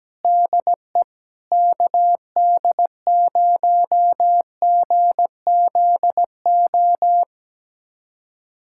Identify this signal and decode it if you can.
Morse code, heard as a keyed tone, about 17 wpm, 700 Hz. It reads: DE KD0GZO
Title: USA Callsign DE Callsign 17wpm